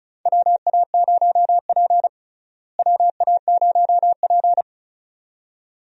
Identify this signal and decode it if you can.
Morse code, heard as a keyed tone, about 36 wpm, 700 Hz. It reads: WA0P WA0P
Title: USA Call Signs - 4 Characters - Repeated 2x - 35wpm